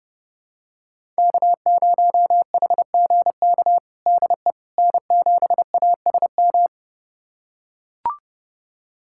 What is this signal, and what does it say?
Morse code, heard as a keyed tone, about 30 wpm, 700 Hz. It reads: K0HGK DE N7ASM
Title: USA Callsign DE Callsign - 5 Characters - 30wpm